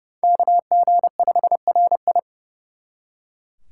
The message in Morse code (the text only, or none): KG5RI